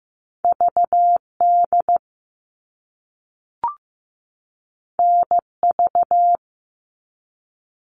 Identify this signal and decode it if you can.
Morse code, heard as a keyed tone, about 15 wpm, 700 Hz. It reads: VD NV
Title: All Cut Numbers - 2-Digit 15wpm